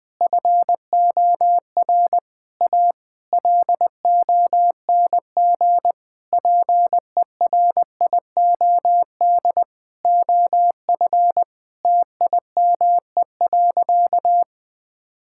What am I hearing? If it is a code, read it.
Morse code, heard as a keyed tone, about 20 wpm, 700 Hz. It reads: FOR A LONG PERIOD OF TIME.
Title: Sentences with 6-Word Phrases - 20wpm